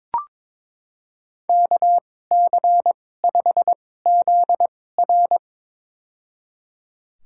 Morse code, heard as a keyed tone, about 22 wpm, spaced as for 16 wpm, 700 Hz. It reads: KC5ZR
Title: USA Call Signs - 5 Characters - 2x Character Spacing - 22wpm